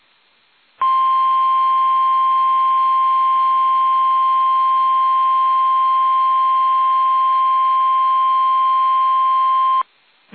Emergency Alert System (SAME) tones (the 1050 Hz attention tone). It is heard from 0.8 to 9.9 s.
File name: eas / KUSO / KUSO